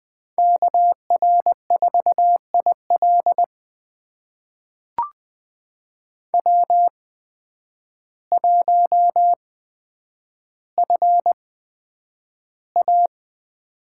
Morse code, heard as a keyed tone, about 20 wpm, spaced as for 9 wpm, 700 Hz. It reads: KR4IL W1FA
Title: USA Call Signs - 8x Character Spacing - 20wpm